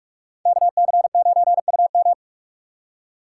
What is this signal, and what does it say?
Morse code, heard as a keyed tone, about 44 wpm, 700 Hz. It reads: KC9UM